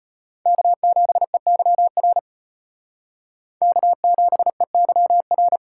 Morse code, heard as a keyed tone, about 32 wpm, 700 Hz. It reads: K7EYR K7EYR